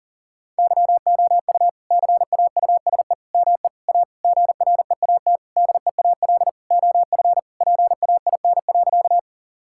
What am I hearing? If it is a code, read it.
Morse code, heard as a keyed tone, about 37 wpm, 700 Hz. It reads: YOU CAUSE ME A GREAT DEAL OF PAIN.
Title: Sentences from Top 1600 Words 40wpm